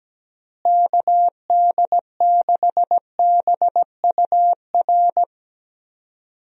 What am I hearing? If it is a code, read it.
Morse code, heard as a keyed tone, about 17 wpm, 700 Hz. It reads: KD6BUR
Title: USA Call Signs - 6 Characters - Repeated 2x - 17wpm